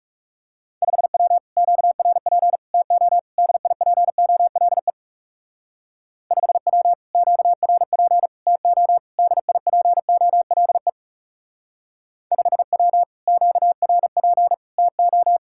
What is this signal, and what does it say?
Morse code, heard as a keyed tone, about 35 wpm, 700 Hz. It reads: 5W QRP TO DIPOLE 5W QRP TO DIPOLE 5W QRP TO
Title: Speed-Racing Common QSO Phrases - 30wpm